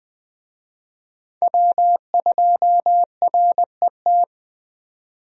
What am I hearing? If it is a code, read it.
Morse code, heard as a keyed tone, about 20 wpm, 700 Hz. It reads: W2RET
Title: USA Call Signs - 5 Characters - 20wpm